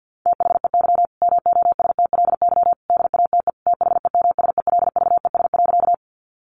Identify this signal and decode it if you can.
Morse code, heard as a keyed tone, about 50 wpm, 700 Hz. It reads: THEY MOSTLY DATE THEMSELVES.
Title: Sentences from Top 1300 Words 50wpm